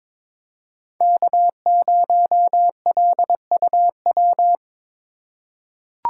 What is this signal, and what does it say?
Morse code, heard as a keyed tone, about 22 wpm, 700 Hz. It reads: K0LUW
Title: K1USN SST QSO Practice - 22wpm